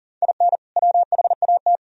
Morse code, heard as a keyed tone, about 38 wpm, 700 Hz. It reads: IN WHAT
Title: Sentences from Top 1000 Words - Review Part 3 - 40wpm